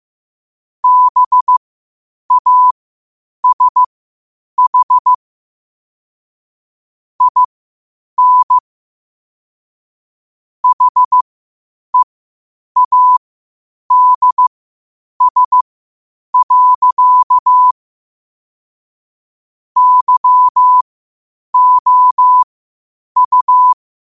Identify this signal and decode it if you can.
Morse code, heard as a keyed tone, about 15 wpm, spaced as for 8 wpm, 1000 Hz. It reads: BASH IN HEADS. YOU
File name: qotdfiles / 8wpm